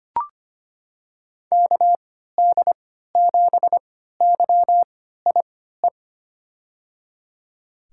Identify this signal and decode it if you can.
Morse code, heard as a keyed tone, about 25 wpm, spaced as for 14 wpm, 700 Hz. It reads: KD7YIE
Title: USA Call Signs - 3x Character Spacing - 25wpm